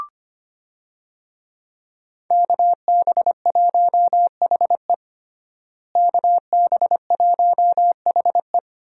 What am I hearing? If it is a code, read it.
Morse code, heard as a keyed tone, about 25 wpm, 700 Hz. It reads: KB1HE KB1HE
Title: USA Call Signs - 5 Characters - Repeated 3x - 25wpm